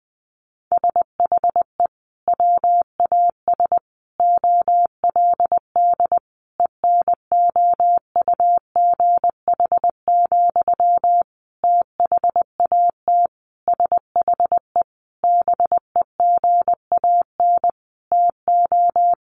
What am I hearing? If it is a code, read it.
Morse code, heard as a keyed tone, about 20 wpm, 700 Hz. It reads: SHE WAS OLD ENOUGH, THAT SHE BEGAN TO
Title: Sentences from Top 800 Words 20wpm